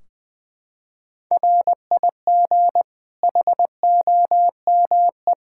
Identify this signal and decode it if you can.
Morse code, heard as a keyed tone, about 20 wpm, 700 Hz. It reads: RIG HOME
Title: Practice QSO - Exchange 2 - Info - 20wpm